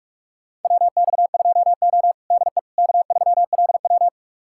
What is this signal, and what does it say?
Morse code, heard as a keyed tone, about 44 wpm, 690 Hz. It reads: WX2O DE K3LW
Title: USA Callsign DE Callsign - 4 Characters - 45wpm